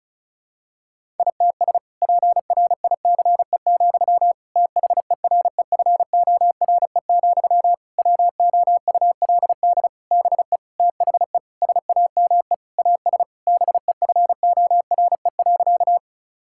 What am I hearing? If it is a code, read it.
Morse code, heard as a keyed tone, about 36 wpm, 700 Hz. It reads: ITS PRICE, THEREFORE, WOULD BE THE SAME AS BEFORE.
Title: Sentences from Top 1000 Words - Review Part 5 - 35wpm